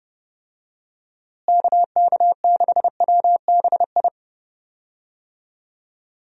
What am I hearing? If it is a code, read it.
Morse code, heard as a keyed tone, about 30 wpm, 700 Hz. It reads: KK6WBI
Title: USA Call Signs - 6 Characters - Repeated 3x - 30wpm